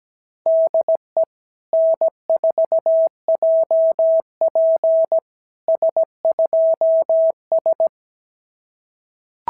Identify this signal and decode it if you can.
Morse code, heard as a keyed tone, about 17 wpm, 650 Hz. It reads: DE N4JP S2S
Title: SOTA QSO Practice 17wpm